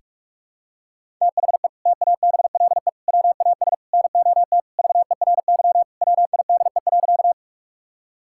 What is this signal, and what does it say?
Morse code, heard as a keyed tone, about 46 wpm, 690 Hz. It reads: THE TABLE WAS NOT VERY WIDE.
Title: Sentences from Top 1000 Words - Review Part 6 - 45wpm